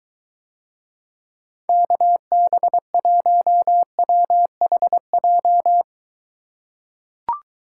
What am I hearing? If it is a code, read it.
Morse code, heard as a keyed tone, about 23 wpm, 700 Hz. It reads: KB1WHJ